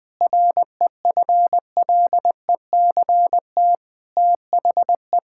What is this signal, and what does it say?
Morse code, heard as a keyed tone, about 21 wpm, 700 Hz. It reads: REFLECT THE
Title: Sentences with 8-Word Phrases - Spoken Beginning - 20wpm